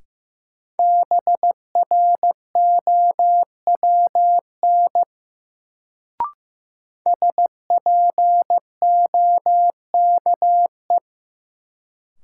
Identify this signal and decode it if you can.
Morse code, heard as a keyed tone, about 15 wpm, 700 Hz. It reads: BROWN SPOKE
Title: Sentences from Top 1200 Words - New Word Drill - 15wpm